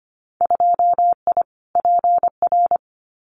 Morse code, heard as a keyed tone, about 24 wpm, 700 Hz. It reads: 2I PR
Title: Winter Field Day Exchanges Encore 25wpm